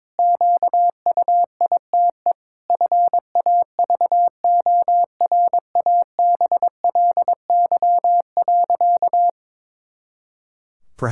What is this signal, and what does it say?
Morse code, heard as a keyed tone, about 22 wpm, 700 Hz. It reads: QUITE FAVORABLY.
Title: Pangram Challenge v2 - 22wpm